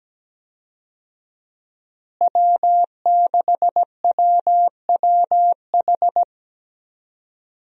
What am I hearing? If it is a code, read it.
Morse code, heard as a keyed tone, about 17 wpm, 700 Hz. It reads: W6WWH
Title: USA Call Signs - 5 Characters - Repeated 3x - 17wpm